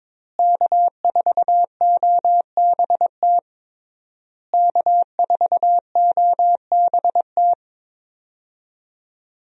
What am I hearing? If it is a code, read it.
Morse code, heard as a keyed tone, about 22 wpm, 700 Hz. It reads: K4OBT K4OBT
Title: USA Call Signs - 5 Characters - Repeated 3x - 22wpm